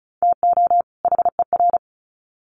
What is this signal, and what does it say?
Morse code, heard as a keyed tone, about 34 wpm, 700 Hz. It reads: TO HER
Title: Sets of 2 Words 35wpm